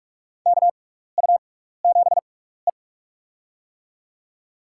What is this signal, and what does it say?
Morse code, heard as a keyed tone, about 42 wpm, spaced as for 15 wpm, 700 Hz. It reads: KU7E
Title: USA Call Signs - 4 Characters - 6x Character Spacing - 45wpm